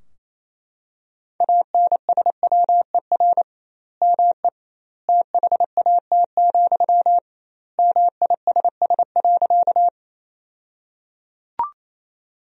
Morse code, heard as a keyed tone, about 28 wpm, 700 Hz. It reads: ANSWER ME THAT, MISS.
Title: 2x Word-Spacing - Sentences from Top 500 Words - 28wpm